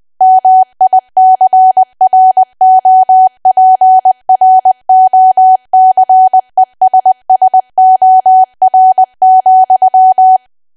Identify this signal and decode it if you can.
Morse code, heard as a keyed tone, about 20 wpm, 750 Hz. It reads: MICROPROCESSOR,
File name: cwa / W1AW-Level3 / 20WPM